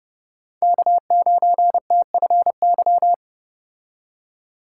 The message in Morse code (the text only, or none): K9TFY